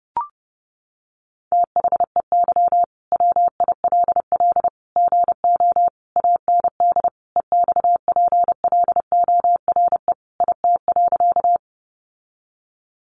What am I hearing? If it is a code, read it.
Morse code, heard as a keyed tone, about 30 wpm, 700 Hz. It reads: THEY WILL GO AND EXPLORE IT.
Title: Sentences from Top 1100 Words 30wpm